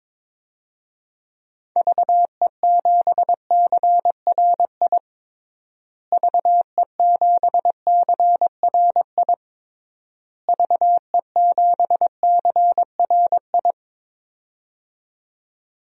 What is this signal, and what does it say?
Morse code, heard as a keyed tone, about 22 wpm, 700 Hz. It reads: VE7CRI VE7CRI VE7CRI
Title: DXCC-100 Call Signs - 6 Characters - Repeated 3x - 22wpm